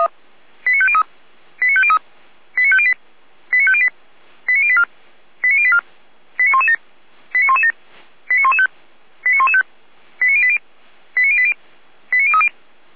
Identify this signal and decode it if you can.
Five-tone selective call (ZVEI), heard as quick runs of five tones, tones of about 70 ms. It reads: 89503 89503 89508 89508 89084 89084 89117 89117 89115 89115 89080 89080 89030
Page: SDR/ZVEI